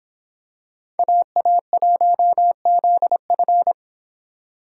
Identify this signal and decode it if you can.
Morse code, heard as a keyed tone, about 26 wpm, 700 Hz. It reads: AA1ZF